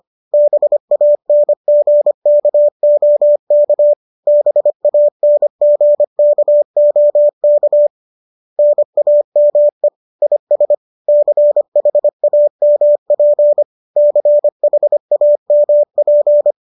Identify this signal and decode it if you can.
Morse code, heard as a keyed tone, about 25 wpm, 590 Hz. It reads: BANGKOK BANGKOK NAME IS CHAMP CHAMP